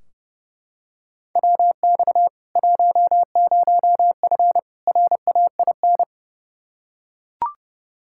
Morse code, heard as a keyed tone, about 30 wpm, 700 Hz. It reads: WX 10F RAIN
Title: K1USN SST QSO Practice 30wpm